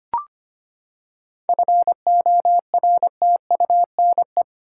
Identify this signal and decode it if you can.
Morse code, heard as a keyed tone, about 25 wpm, 700 Hz. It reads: FORTUNE
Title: Seven Letter Words - 25wpm